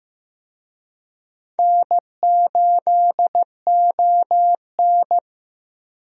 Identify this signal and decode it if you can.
Morse code, heard as a keyed tone, about 15 wpm, 700 Hz. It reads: N8ON
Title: USA Call Signs - 4 Characters - 15wpm